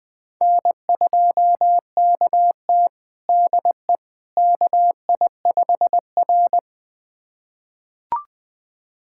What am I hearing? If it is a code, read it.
Morse code, heard as a keyed tone, about 20 wpm, 700 Hz. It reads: N2KT DE KI5R
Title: USA Callsign DE Callsign - 4 Characters - 20wpm